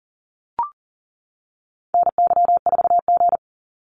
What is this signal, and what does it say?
Morse code, heard as a keyed tone, about 40 wpm, 700 Hz. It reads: NY4G
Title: NA SOTA Activator Callsigns 40wpm